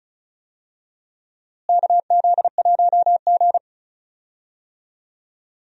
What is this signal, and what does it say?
Morse code, heard as a keyed tone, about 36 wpm, 700 Hz. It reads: KZ1G
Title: USA Call Signs - 4 Characters - Repeated 3x - 35wpm